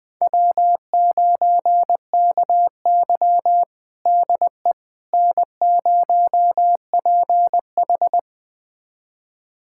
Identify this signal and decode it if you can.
Morse code, heard as a keyed tone, about 20 wpm, 700 Hz. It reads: W9KY DE N0PH
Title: USA Callsign DE Callsign - 4 Characters - 20wpm